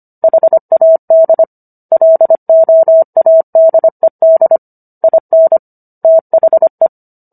Morse code, heard as a keyed tone, about 25 wpm, 650 Hz. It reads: HAD LOADED IN THE